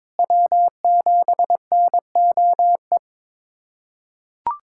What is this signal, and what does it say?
Morse code, heard as a keyed tone, about 22 wpm, 700 Hz. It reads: W7NOE
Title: USA Callsign DE Callsign - 5 Characters - 22wpm